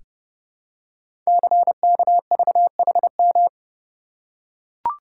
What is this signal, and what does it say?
Morse code, heard as a keyed tone, about 29 wpm, 700 Hz. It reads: CKVHM
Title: Code Groups - 5 Letters - 30wpm